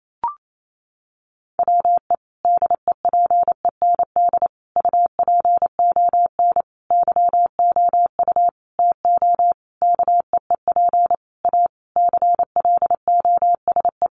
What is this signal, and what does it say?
Morse code, heard as a keyed tone, about 28 wpm, 700 Hz. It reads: WE DEPEND UPON YOU TO KEEP A CLOSE
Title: Sentences from Top 1300 Words 28wpm